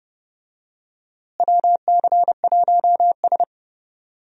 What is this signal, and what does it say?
Morse code, heard as a keyed tone, about 30 wpm, 700 Hz. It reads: WC1S
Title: USA Call Signs - 4 Characters - Repeated 3x - 30wpm